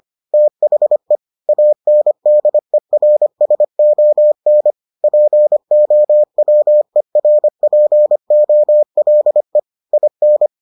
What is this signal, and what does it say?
Morse code, heard as a keyed tone, about 25 wpm, 600 Hz. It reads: THE ANDERSON POWERPOLE IN